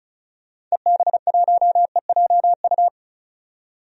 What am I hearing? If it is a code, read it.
Morse code, heard as a keyed tone, about 34 wpm, 700 Hz. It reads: EB1EJU
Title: DXCC-100 Call Signs - 6 Characters - 35wpm